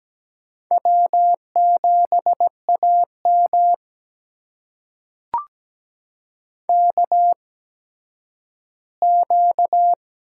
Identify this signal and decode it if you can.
Morse code, heard as a keyed tone, about 17 wpm, spaced as for 10 wpm, 700 Hz. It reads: W7AM KQ
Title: USA Call Signs - 4 Characters - 8x Character Spacing - 17wpm